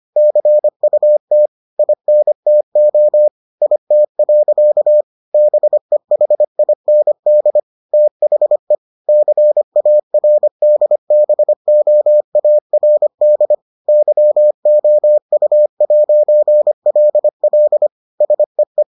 Morse code, heard as a keyed tone, about 25 wpm, 600 Hz. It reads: CUT INTO IT. BEHIND THE CARDBOARD YOU'LL SEE